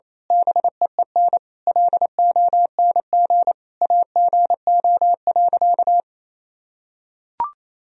Morse code, heard as a keyed tone, about 28 wpm, 700 Hz. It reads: BEEN LONG AGO.
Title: Sentences from Top 1000 Words - Review Part 3 - 28wpm